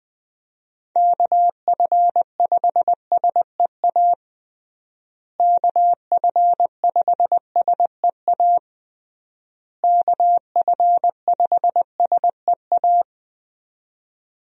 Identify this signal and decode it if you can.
Morse code, heard as a keyed tone, about 21 wpm, 700 Hz. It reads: KF5SEA KF5SEA KF5SEA
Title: USA Call Signs - 6 Characters - Repeated 3x - 20wpm